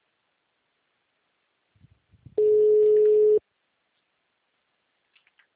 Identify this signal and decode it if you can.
Ringback tone (Europe, 425 Hz).
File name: out- -2011- - - .43601